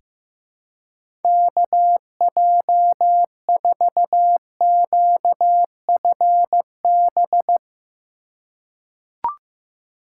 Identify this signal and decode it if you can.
Morse code, heard as a keyed tone, about 15 wpm, 700 Hz. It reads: KJ4QFB
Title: USA Call Signs - 2x Character Spacing - 15wpm